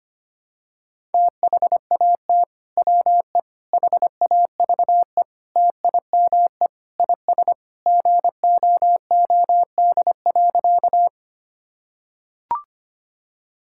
Morse code, heard as a keyed tone, about 25 wpm, 700 Hz. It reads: THAT WE HAVE TIME IS GOOD.
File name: practice / 25wpm / Sentences with 3-Word Phrases - 25wpm